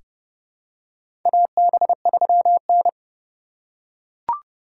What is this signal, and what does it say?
Morse code, heard as a keyed tone, about 30 wpm, 700 Hz. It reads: AB3N
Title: USA Call Signs - 4 Characters - 3x Character Spacing - 30wpm